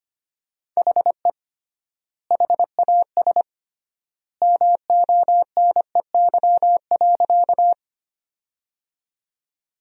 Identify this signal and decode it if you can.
Morse code, heard as a keyed tone, about 25 wpm, 700 Hz. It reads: HE HAS MONEY.
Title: 3x Word-Spacing - Sentences from Top 400 Words - 25wpm